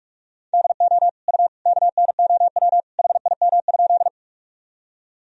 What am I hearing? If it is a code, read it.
Morse code, heard as a keyed tone, about 46 wpm, 700 Hz. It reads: DO U KNOW HIM?
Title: Sentences from Top 100 Words and Abbreviations 45wpm